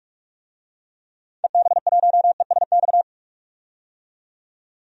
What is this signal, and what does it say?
Morse code, heard as a keyed tone, about 47 wpm, 700 Hz. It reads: EB1ESX
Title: DXCC-100 Call Signs - 6 Characters - 45wpm